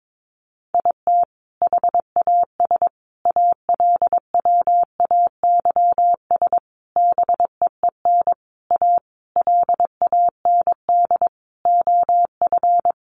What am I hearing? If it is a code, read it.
Morse code, heard as a keyed tone, about 22 wpm, 700 Hz. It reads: IT HAS ALWAYS BEEN A LAND OF
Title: Sentences from Top 1300 Words 22wpm